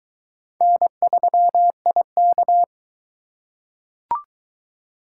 Morse code, heard as a keyed tone, about 23 wpm, 700 Hz. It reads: N3IK